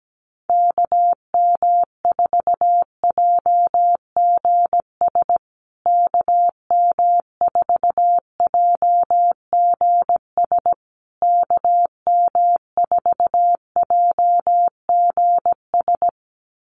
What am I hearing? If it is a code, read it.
Morse code, heard as a keyed tone, about 17 wpm, 700 Hz. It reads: KM4JGS KM4JGS KM4JGS
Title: USA Call Signs Repeated 3x 17wpm